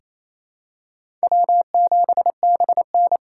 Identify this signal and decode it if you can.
Morse code, heard as a keyed tone, about 28 wpm, 700 Hz. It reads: W7BN
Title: USA Call Signs - 4 Characters - 4x Character Spacing - 28wpm